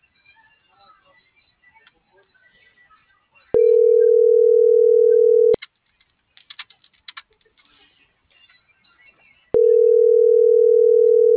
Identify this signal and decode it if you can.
Ringback tone (North America), 440 + 480 Hz.